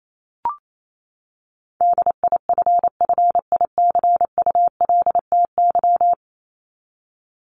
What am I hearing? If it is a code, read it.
Morse code, heard as a keyed tone, about 28 wpm, 700 Hz. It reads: DIFFICULTY
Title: Top 1251-1500 Words 28wpm